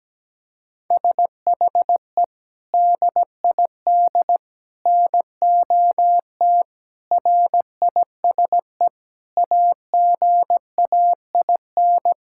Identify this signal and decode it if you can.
Morse code, heard as a keyed tone, about 17 wpm, 700 Hz. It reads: SHE DID NOT RISE AGAIN
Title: Sentences from Top 1000 Words - Review Part 6 - 17wpm